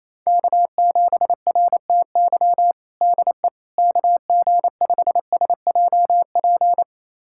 Morse code, heard as a keyed tone, about 28 wpm, 700 Hz. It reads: K7RTY DE KG5SJP
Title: USA Callsign DE Callsign 28wpm